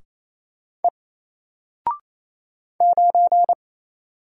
Morse code, heard as a keyed tone, about 28 wpm, 700 Hz. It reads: E 9
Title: Single Letter-Number Alt 28wpm